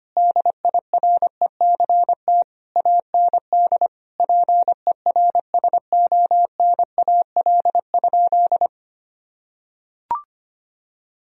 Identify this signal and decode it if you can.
Morse code, heard as a keyed tone, about 25 wpm, 700 Hz. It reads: DIRECT AND PERSONAL?
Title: Sentences from Top 1000 Words - Review Part 4 - 25wpm